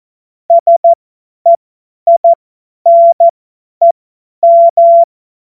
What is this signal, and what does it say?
Morse code, heard as a keyed tone, about 14 wpm, spaced as for 10 wpm, 680 Hz. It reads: SEINEM
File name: Witz-07-Oktober-2025-50-F70